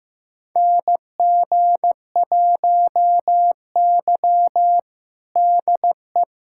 Morse code, heard as a keyed tone, about 15 wpm, 700 Hz. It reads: NG1Y DE
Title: USA Callsign DE Callsign - 4 Characters - 15wpm